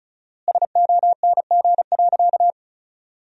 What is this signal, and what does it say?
Morse code, heard as a keyed tone, about 36 wpm, 700 Hz. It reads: SONG.
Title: 4x Word-Spacing - Sentences from Top 400 Words - 35wpm